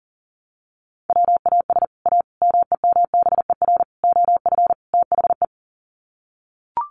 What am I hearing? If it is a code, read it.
Morse code, heard as a keyed tone, about 37 wpm, 700 Hz. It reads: WAS A MEMBER OF THE
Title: Sets of 5 Words 40wpm